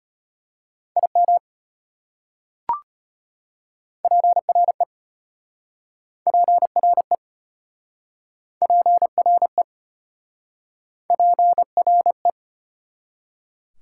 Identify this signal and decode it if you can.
Morse code, heard as a keyed tone, about 33 wpm, 700 Hz. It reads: IM PRE PRE PRE PRE